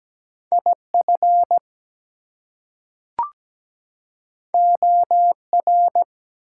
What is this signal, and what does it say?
Morse code, heard as a keyed tone, about 17 wpm, 700 Hz. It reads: IF OR
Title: Two Letter Words - 17wpm